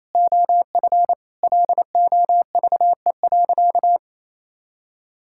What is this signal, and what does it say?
Morse code, heard as a keyed tone, about 28 wpm, 700 Hz. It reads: OF LOVE.
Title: Sentences from Top 1000 Words - Review Part 5 - 28wpm